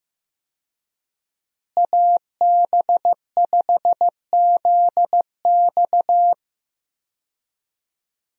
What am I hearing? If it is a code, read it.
Morse code, heard as a keyed tone, about 15 wpm, 700 Hz. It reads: AB5ZX
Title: USA Call Signs - 5 Characters - Repeated 3x - 15wpm